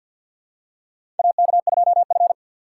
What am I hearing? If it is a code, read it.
Morse code, heard as a keyed tone, about 50 wpm, 700 Hz. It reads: AK2F
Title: USA Call Signs - 4x Character Spacing - 50wpm